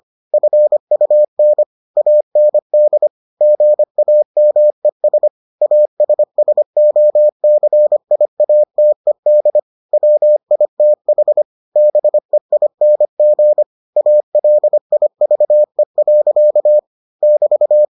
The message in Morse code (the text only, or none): FUN AND GAMES ASSOCIATED WITH BEING ALIVE. <BT>